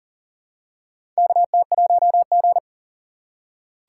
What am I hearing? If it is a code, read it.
Morse code, heard as a keyed tone, about 38 wpm, 700 Hz. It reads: KT1G
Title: USA Call Signs - 4 Characters - 4x Character Spacing - 40wpm